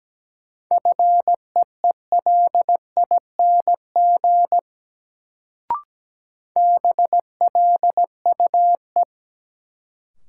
Morse code, heard as a keyed tone, about 17 wpm, 700 Hz. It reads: FEELING BLUE
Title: Sentences from Top 1000 Words - Review Part 3 - New Word Drill - 17wpm